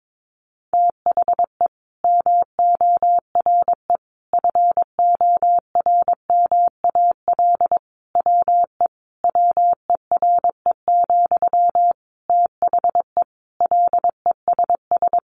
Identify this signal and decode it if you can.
Morse code, heard as a keyed tone, about 22 wpm, 700 Hz. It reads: THE MORE FORMAL WE WERE, THE LESS
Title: Sentences from Top 1300 Words 22wpm